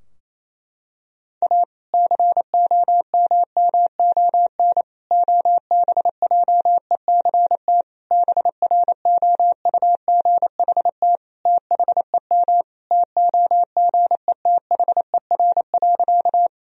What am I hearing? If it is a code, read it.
Morse code, heard as a keyed tone, about 28 wpm, 700 Hz. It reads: A COMMON OBJECT BROUGHT THEM TOGETHER.
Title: Sentences from Top 1000 Words - Review Part 3 - 28wpm